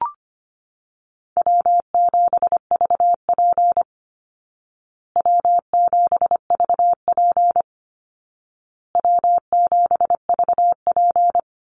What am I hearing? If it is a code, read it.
Morse code, heard as a keyed tone, about 25 wpm, 700 Hz. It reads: W7VP W7VP W7VP
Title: Call Signs Repeated 2x - 25wpm